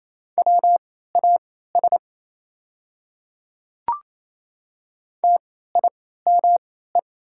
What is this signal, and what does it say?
Morse code, heard as a keyed tone, about 28 wpm, spaced as for 16 wpm, 700 Hz. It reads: WAS TIME
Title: 3x Character Spacing - Top 100 Words - 28wpm